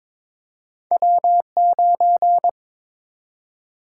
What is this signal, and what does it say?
Morse code, heard as a keyed tone, about 23 wpm, 700 Hz. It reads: W9
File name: practice / 22wpm / Two Letter-Number Combinations - 22wpm